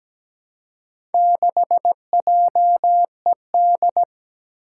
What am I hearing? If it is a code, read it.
Morse code, heard as a keyed tone, about 17 wpm, 700 Hz. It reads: 6JED
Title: Four Challenging Letter-Number Combinations 17wpm